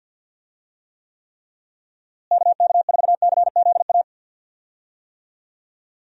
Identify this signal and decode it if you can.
Morse code, heard as a keyed tone, about 50 wpm, 700 Hz. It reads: KK4CZA
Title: USA Call Signs - 6 Characters - Repeated 3x - 50wpm